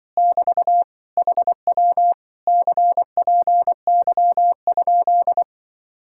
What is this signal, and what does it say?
Morse code, heard as a keyed tone, about 24 wpm, 700 Hz. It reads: <BT> HW CPY?
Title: Speed-Racing Common QSO Phrases Encore - 20wpm